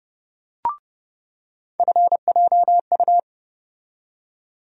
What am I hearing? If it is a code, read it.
Morse code, heard as a keyed tone, about 29 wpm, 700 Hz. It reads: FJU
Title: USA Call Sign Postfixes - 3 Letters 30wpm